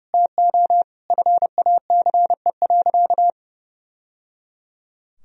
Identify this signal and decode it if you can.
Morse code, heard as a keyed tone, about 30 wpm, 700 Hz. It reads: TO FACE.
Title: Sentences from Top 1000 Words - Review Part 2 - 30wpm